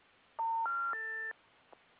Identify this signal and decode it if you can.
Special information tone.
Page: S-series, P-series, R-series, and M-series recordings were recorded using a pickup coil. No digital amplification was performed, to preserve the original levels.